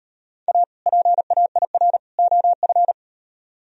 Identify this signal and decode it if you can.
Morse code, heard as a keyed tone, about 38 wpm, 700 Hz. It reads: A PAIR OF